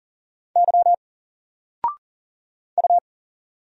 Morse code, heard as a keyed tone, about 40 wpm, 700 Hz. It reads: Y U
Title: Single Letter-Number Alt 40wpm